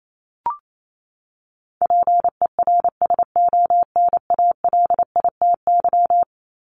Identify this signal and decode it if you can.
Morse code, heard as a keyed tone, about 28 wpm, 700 Hz. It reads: PERSONALITY
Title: Top 1251-1500 Words 28wpm